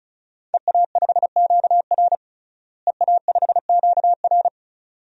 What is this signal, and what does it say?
Morse code, heard as a keyed tone, about 36 wpm, 700 Hz. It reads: EA5QR EA5QR
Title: DXCC-100 Call Signs - 5 Characters - Repeated 2x - 35wpm